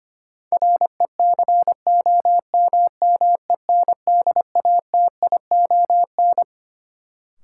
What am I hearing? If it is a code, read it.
Morse code, heard as a keyed tone, about 25 wpm, 700 Hz. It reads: RECOMMENDATION
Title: Sentences from Top 1400 Words - New Word Drill - 25wpm